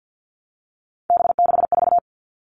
Morse code, heard as a keyed tone, about 50 wpm, 700 Hz. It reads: B64
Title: All Cut Numbers - 3-Digit 50wpm